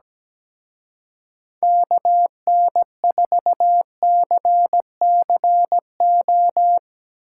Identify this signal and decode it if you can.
Morse code, heard as a keyed tone, about 18 wpm, 700 Hz. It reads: KN4CCO
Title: Speed Racing - USA Call Signs 17wpm